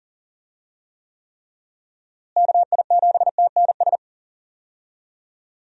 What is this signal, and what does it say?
Morse code, heard as a keyed tone, about 40 wpm, 700 Hz. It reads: KI7TNS
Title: USA Call Signs - 6 Characters - Repeated 3x - 40wpm